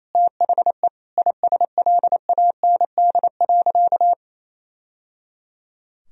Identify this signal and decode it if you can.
Morse code, heard as a keyed tone, about 28 wpm, 700 Hz. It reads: THE ISLAND.
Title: Sentences from Top 1000 Words - Review Part 3 - 28wpm